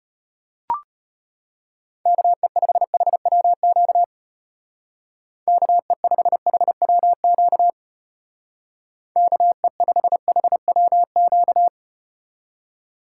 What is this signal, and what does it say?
Morse code, heard as a keyed tone, about 33 wpm, 700 Hz. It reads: KE5HWQ KE5HWQ KE5HWQ